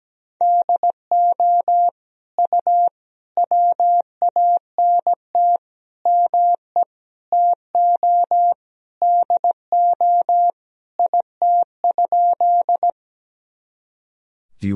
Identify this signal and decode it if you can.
Morse code, heard as a keyed tone, about 17 wpm, 700 Hz. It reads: DO U WANT ME TO DO IT?
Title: Sentences from Top 200 Words and Abbreviations 17wpm